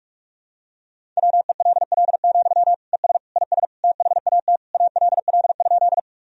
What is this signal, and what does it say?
Morse code, heard as a keyed tone, about 46 wpm, 700 Hz. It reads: WELL, ES IS THAT ALL?
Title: Sentences from Top 200 Words and Abbreviations 45wpm